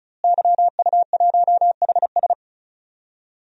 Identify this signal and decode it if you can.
Morse code, heard as a keyed tone, about 36 wpm, 700 Hz. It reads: YU1HS
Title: DXCC-100 Call Signs - 5 Characters - Repeated 2x - 35wpm